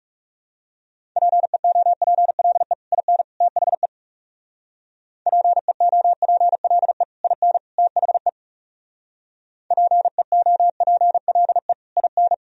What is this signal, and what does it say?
Morse code, heard as a keyed tone, about 37 wpm, 700 Hz. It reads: PEOPLE IN THE PEOPLE IN THE PEOPLE IN
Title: Speed-Racing - Sets of 3 Words 30wpm